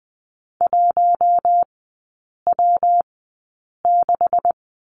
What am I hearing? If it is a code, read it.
Morse code, heard as a keyed tone, about 20 wpm, spaced as for 8 wpm, 700 Hz. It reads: 1W6
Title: 127 Lesson 21e - Number 6 - Three Characters 20wpm